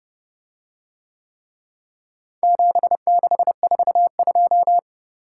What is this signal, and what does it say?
Morse code, heard as a keyed tone, about 29 wpm, 700 Hz. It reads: 7642